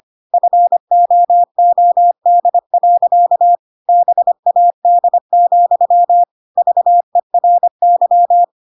Morse code, heard as a keyed tone, about 25 wpm, 700 Hz. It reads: FOOD. BAD, VERY
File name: Pinocchio / Pinocchio ch7 25 25 ik1hge